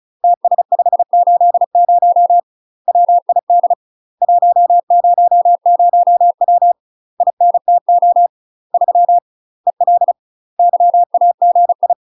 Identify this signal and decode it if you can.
Morse code, heard as a keyed tone, about 35 wpm, 700 Hz. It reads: TS580 WID 100W INTO 3 EL YAGI
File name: UK MASTER QSOs TRAINING / 35WPM / UK MASTER QSOs 0007